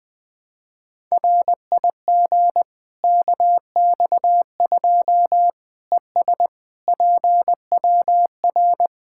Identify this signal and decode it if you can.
Morse code, heard as a keyed tone, about 20 wpm, 700 Hz. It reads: RIG KX2 ES PWR
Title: Practice QSO - Exchange 2 - Info - 20wpm